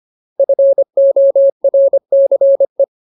Morse code, heard as a keyed tone, about 25 wpm, 550 Hz. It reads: FORCE
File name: Ultimate Word List / 25WPM Ultimate